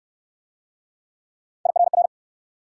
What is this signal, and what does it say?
Morse code, heard as a keyed tone, about 58 wpm, 700 Hz. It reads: IRR